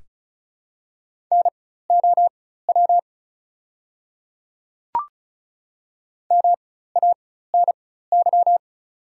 Morse code, heard as a keyed tone, about 36 wpm, spaced as for 16 wpm, 690 Hz. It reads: NOW MANY
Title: 4x Character Spacing - Top 500 Words - 35wpm